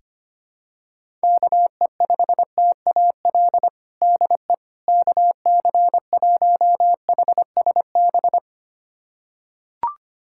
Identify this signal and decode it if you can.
Morse code, heard as a keyed tone, about 25 wpm, 700 Hz. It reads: KE5TAL DE KC1HSB
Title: USA Callsign DE Callsign - 6 Characters - 25wpm